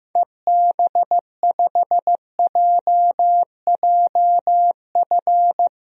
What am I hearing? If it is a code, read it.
Morse code, heard as a keyed tone, about 16 wpm, 700 Hz. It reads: EB5JJF
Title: DXCC-100 Call Signs - 6 Characters - Repeated 2x - 15wpm